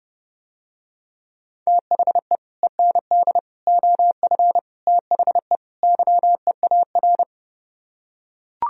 Morse code, heard as a keyed tone, about 30 wpm, 700 Hz. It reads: THE END OF THE YEAR